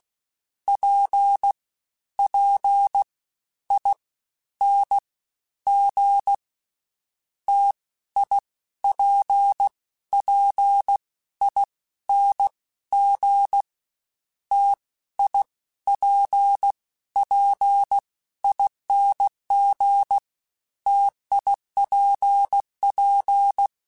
Morse code, heard as a keyed tone, about 16 wpm, 800 Hz. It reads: P P I N G T I P P I N G T I P P ING TIPP
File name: Difficult Combos / Difficult Combos-mp3s / T AOEI @15